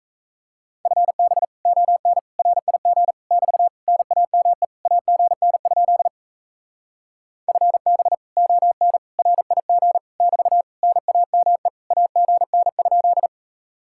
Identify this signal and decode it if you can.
Morse code, heard as a keyed tone, about 37 wpm, 700 Hz. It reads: FB ON RIG <BT> NAME AGN? FB ON RIG <BT> NAME AGN?